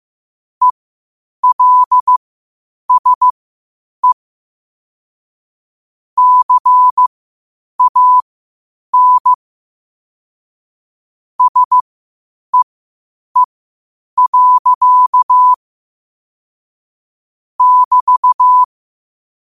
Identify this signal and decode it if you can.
Morse code, heard as a keyed tone, about 15 wpm, spaced as for 8 wpm, 1000 Hz. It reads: ELSE CAN SEE. <BT>
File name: qotdfiles / 8wpm